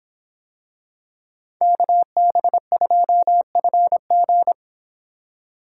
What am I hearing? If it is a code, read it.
Morse code, heard as a keyed tone, about 25 wpm, 700 Hz. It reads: KB2FG